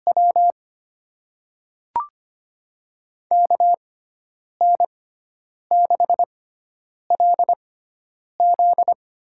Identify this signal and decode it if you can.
Morse code, heard as a keyed tone, about 25 wpm, spaced as for 9 wpm, 700 Hz. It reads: W KN6LZ
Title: USA Call Signs - 5 Characters - 6x Character Spacing - 25wpm